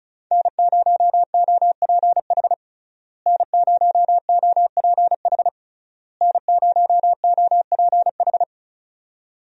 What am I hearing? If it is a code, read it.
Morse code, heard as a keyed tone, about 36 wpm, 700 Hz. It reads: N0OPH N0OPH N0OPH
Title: USA Call Signs - 5 Characters - Repeated 3x - 35wpm